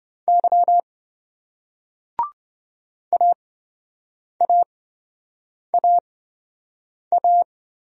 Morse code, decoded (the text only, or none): Y AAAA